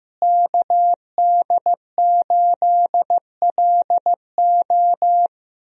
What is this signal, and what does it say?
Morse code, heard as a keyed tone, about 15 wpm, 700 Hz. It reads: KD8LO
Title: USA Call Signs - 5 Characters - Repeated 3x - 15wpm